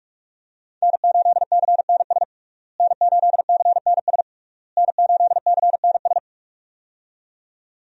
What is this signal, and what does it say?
Morse code, heard as a keyed tone, about 42 wpm, 700 Hz. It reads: N8CNS N8CNS N8CNS
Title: USA Call Signs - 5 Characters - Repeated 3x - 45wpm